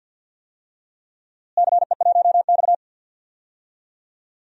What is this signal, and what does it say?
Morse code, heard as a keyed tone, about 53 wpm, 700 Hz. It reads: CE1X
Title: DXCC-100 Call Signs - 4 Characters - Repeated 2x - 50wpm